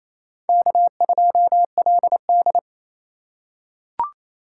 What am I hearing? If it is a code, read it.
Morse code, heard as a keyed tone, about 28 wpm, 700 Hz. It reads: K2LD
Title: USA Call Signs - 4 Characters - 8x Character Spacing - 28wpm